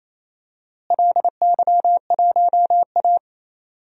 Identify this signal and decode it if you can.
Morse code, heard as a keyed tone, about 28 wpm, 700 Hz. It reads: LY1A
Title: DXCC-100 Call Signs - 4 Characters - Repeated 2x - 28wpm